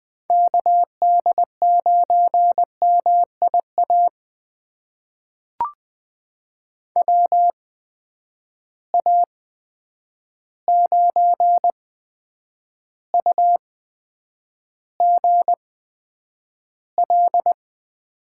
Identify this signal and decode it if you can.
Morse code, heard as a keyed tone, about 20 wpm, spaced as for 9 wpm, 700 Hz. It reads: KD9MIA WA9UGL
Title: USA Call Signs - 6 Characters - 8x Character Spacing - 20wpm